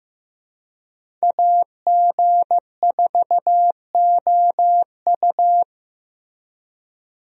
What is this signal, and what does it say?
Morse code, heard as a keyed tone, about 15 wpm, 700 Hz. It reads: AG4OU
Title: USA Call Signs - 5 Characters - Repeated 3x - 15wpm